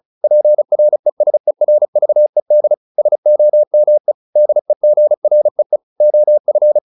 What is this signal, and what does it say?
Morse code, heard as a keyed tone, about 34 wpm, 600 Hz. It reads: PRESERVED SOME DEGREE OF